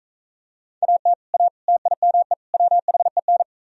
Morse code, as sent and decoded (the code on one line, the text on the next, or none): .- - / .- / - .. -- . / .-- .... -.
AT A TIME WHEN